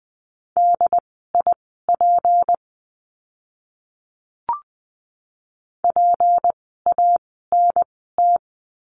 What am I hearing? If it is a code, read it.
Morse code, heard as a keyed tone, about 20 wpm, spaced as for 14 wpm, 700 Hz. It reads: DIP PANT
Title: 153 Lesson 25f - Letter p - Words 20wpm